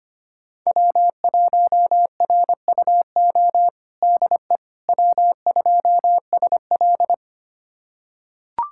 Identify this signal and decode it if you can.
Morse code, heard as a keyed tone, about 25 wpm, 700 Hz. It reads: W1RUO DE W2SL
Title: USA Callsign DE Callsign 25wpm